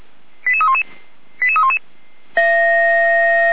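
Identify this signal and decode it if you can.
Five-tone selective call (ZVEI), heard as quick runs of five tones, tones of about 70 ms. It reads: 88311 88311
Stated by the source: Received on an SDR, logged by Feuerwehr Orlamünde (fire brigade).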